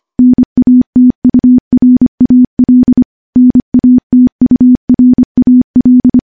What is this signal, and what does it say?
Morse code, heard as a keyed tone, about 25 wpm, 250 Hz. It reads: NATURAL NATURAL
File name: morse-training-25WPM